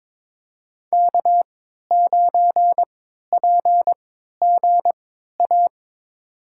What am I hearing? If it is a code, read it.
Morse code, heard as a keyed tone, about 22 wpm, spaced as for 13 wpm, 700 Hz. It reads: K9PGA
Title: USA Call Signs - 5 Characters - 3x Character Spacing - 22wpm